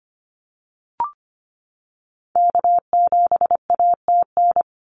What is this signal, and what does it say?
Morse code, heard as a keyed tone, about 25 wpm, 700 Hz. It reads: K7ATN
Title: NA SOTA Activator Callsigns 25wpm